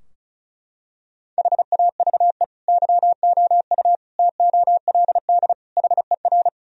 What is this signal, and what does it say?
Morse code, heard as a keyed tone, about 34 wpm, 700 Hz. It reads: HAVE YOU TOLD HER
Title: Sentences from Top 1000 Words - Review Part 2 - 35wpm